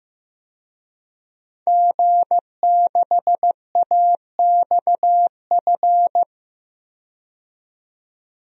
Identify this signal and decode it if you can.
Morse code, heard as a keyed tone, about 15 wpm, 700 Hz. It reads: G6AXF
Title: DXCC-100 Call Signs - 5 Characters - 15wpm